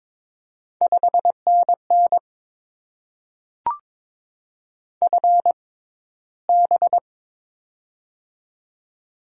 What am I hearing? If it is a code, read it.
Morse code, heard as a keyed tone, about 22 wpm, spaced as for 13 wpm, 690 Hz. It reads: 5NN FB
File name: Top 20 QSO Elements - 6x Character Spacing - 22wpm